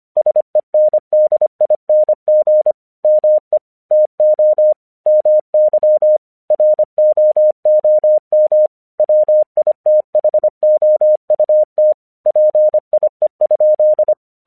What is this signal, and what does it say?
Morse code, heard as a keyed tone, about 26 wpm, 610 Hz. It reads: SENDING ME TO MY ROOM WITHOUT PIE?
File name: 1 English 25wpm / AQC21